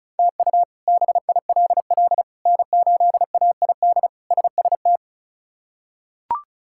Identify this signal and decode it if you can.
Morse code, heard as a keyed tone, about 34 wpm, 700 Hz. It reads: TU BILL N8AID SST
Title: K1USN SST QSO Practice - 35wpm